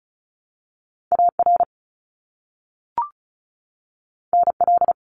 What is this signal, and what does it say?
Morse code, heard as a keyed tone, about 36 wpm, 700 Hz. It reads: AR NL
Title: ARRL Sections 35wpm